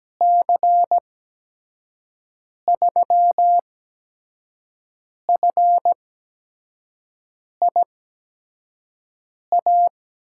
Morse code, heard as a keyed tone, about 17 wpm, spaced as for 5 wpm, 700 Hz. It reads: C3FIA
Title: USA Call Signs - 6 Characters - 8x Character Spacing - 17wpm